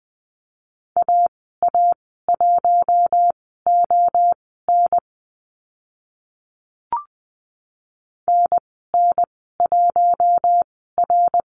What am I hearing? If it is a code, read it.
Morse code, heard as a keyed tone, about 19 wpm, spaced as for 14 wpm, 700 Hz. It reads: AA1ON NN1R
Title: 055 Lesson 10g - Letter r - Call Signs 20wpm